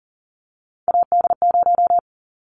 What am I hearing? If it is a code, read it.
Morse code, heard as a keyed tone, about 40 wpm, 700 Hz. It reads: AD0
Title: Callsign Prefixes for Top 25 DXCC - 40wpm